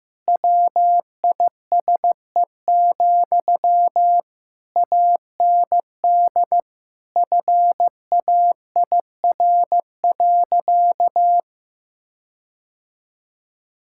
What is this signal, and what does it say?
Morse code, heard as a keyed tone, about 15 wpm, 700 Hz. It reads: WISE, AND FAIR.
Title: Sentences from Top 1600 Words 15wpm